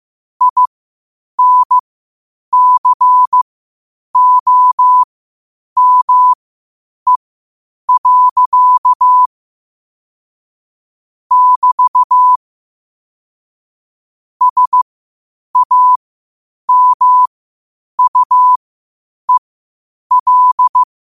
Morse code, heard as a keyed tone, about 15 wpm, spaced as for 8 wpm, 1000 Hz. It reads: INCOME. <BT> SAMUEL
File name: qotdfiles / 8wpm